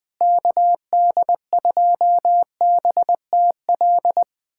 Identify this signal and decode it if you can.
Morse code, heard as a keyed tone, about 21 wpm, 700 Hz. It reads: KD2BTL
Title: USA Callsign DE Callsign - 6 Characters - 20wpm